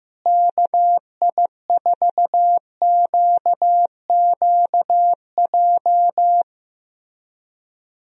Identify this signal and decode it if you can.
Morse code, heard as a keyed tone, about 15 wpm, 700 Hz. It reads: KI4QQJ
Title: USA Call Signs - 6 Characters - Repeated 3x - 15wpm